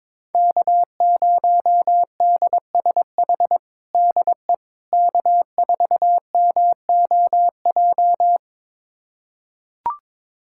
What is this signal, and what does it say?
Morse code, heard as a keyed tone, about 21 wpm, 700 Hz. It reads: K0DSH DE K4MOJ
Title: USA Callsign DE Callsign - 5 Characters - 22wpm